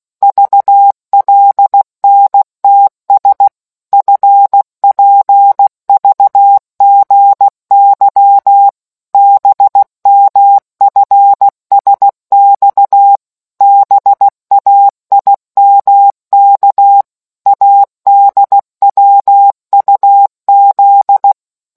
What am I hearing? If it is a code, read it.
Morse code, heard as a keyed tone, about 16 wpm, 800 Hz. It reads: VLNTS FPVGY BMFSX BAIMK ADWUZ